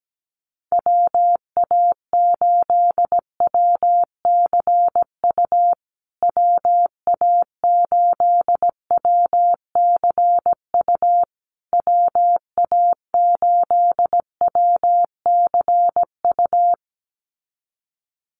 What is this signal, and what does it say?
Morse code, heard as a keyed tone, about 17 wpm, 700 Hz. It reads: WA8WCU WA8WCU WA8WCU
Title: USA Call Signs Repeated 3x 17wpm